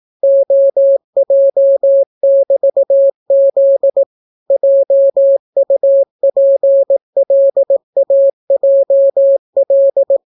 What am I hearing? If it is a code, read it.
Morse code, heard as a keyed tone, about 18 wpm, 550 Hz. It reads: OJ<BT>Z JUPLAJL